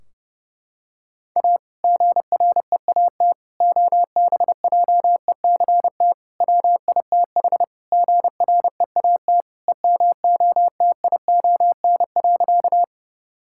Morse code, heard as a keyed tone, about 30 wpm, 700 Hz. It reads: A GREAT OBJECT WITH GREAT EMOTION.
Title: Sentences from Top 1600 Words 30wpm